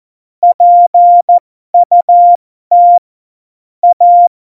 Morse code, heard as a keyed tone, about 14 wpm, spaced as for 12 wpm, 700 Hz. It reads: PUT A
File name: Morsekurs / Witze